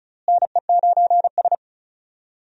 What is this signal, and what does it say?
Morse code, heard as a keyed tone, about 36 wpm, 700 Hz. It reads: NE9S
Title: USA Call Signs - 4 Characters - Repeated 2x - 35wpm